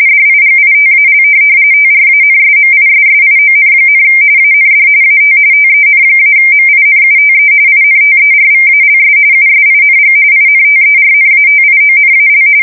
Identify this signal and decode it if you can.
RTTY, about 45.45 baud. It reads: DIGITAL TRANSMISSION, AND AT TEN-MINUTE INTERVALS USING INTERNATIONAL
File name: fsk170 / teletype-RTTY